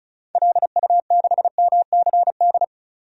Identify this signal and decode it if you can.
Morse code, heard as a keyed tone, about 36 wpm, 700 Hz. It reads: LU6MCD
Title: DXCC-100 Call Signs - 6 Characters - 35wpm